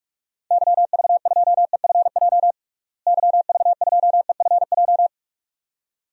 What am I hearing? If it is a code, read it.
Morse code, heard as a keyed tone, about 46 wpm, 700 Hz. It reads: YV2EFJ YV2EFJ
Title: DXCC-100 Call Signs - 6 Characters - Repeated 2x - 45wpm